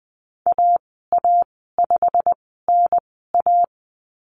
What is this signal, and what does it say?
Morse code, heard as a keyed tone, about 20 wpm, spaced as for 14 wpm, 700 Hz. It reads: AA5NA
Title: 128 Lesson 21g - Number 6 - Call Signs 20wpm